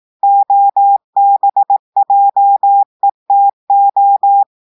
Morse code, heard as a keyed tone, about 18 wpm, 800 Hz. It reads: OBJETO
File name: audios / portugais / portugais texte en clair 1 vitesse 18 mots minutes